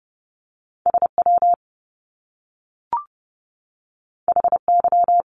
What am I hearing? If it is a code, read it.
Morse code, heard as a keyed tone, about 30 wpm, 700 Hz. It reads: SW HY
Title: Top Two Letter Combinations in Words 30wpm